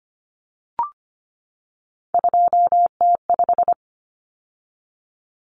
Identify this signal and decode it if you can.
Morse code, heard as a keyed tone, about 25 wpm, 700 Hz. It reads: 2T5
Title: Common Cut Numbers - 3-Digit 25wpm